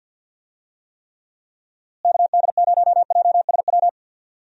Morse code, heard as a keyed tone, about 53 wpm, 700 Hz. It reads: KD0JSW
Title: USA Call Signs - 6 Characters - Repeated 3x - 50wpm